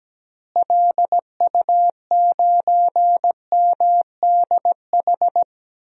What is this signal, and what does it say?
Morse code, heard as a keyed tone, about 17 wpm, 700 Hz. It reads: LU9MDH
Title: DXCC-100 Call Signs - 6 Characters - Repeated 2x - 17wpm